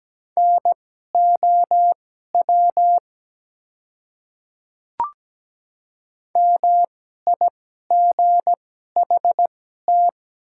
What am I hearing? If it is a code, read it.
Morse code, heard as a keyed tone, about 17 wpm, spaced as for 12 wpm, 700 Hz. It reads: NOW MIGHT
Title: 2x Character Spacing - Top 300 Words - 17wpm